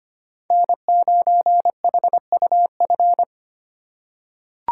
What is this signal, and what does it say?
Morse code, heard as a keyed tone, about 25 wpm, 700 Hz. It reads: N9HUF
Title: USA Call Signs - 5 Characters - 2x Character Spacing - 25wpm